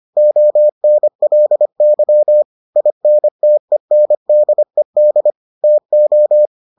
Morse code, heard as a keyed tone, about 25 wpm, 600 Hz. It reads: ONLY INTENDED TO